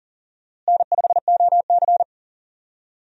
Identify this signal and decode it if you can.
Morse code, heard as a keyed tone, about 40 wpm, 700 Hz. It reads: N5OC
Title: USA Call Signs - 4 Characters - Repeated 2x - 40wpm